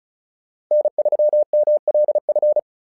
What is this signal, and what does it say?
Morse code, heard as a keyed tone, about 34 wpm, 600 Hz. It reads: N3MLF
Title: SOTA QSO Practice 35wpm